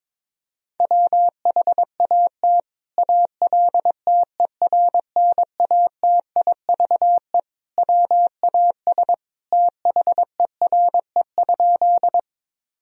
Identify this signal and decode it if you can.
Morse code, heard as a keyed tone, about 22 wpm, 700 Hz. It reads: WHAT ALTERNATIVE WAS THERE?
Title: Sentences from Top 1000 Words - Review Part 4 - 22wpm